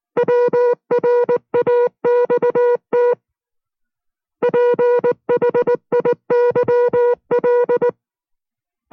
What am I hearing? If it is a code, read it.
Morse code, heard as a keyed tone, about 19 wpm, 480 Hz. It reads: WRAXT PHIYL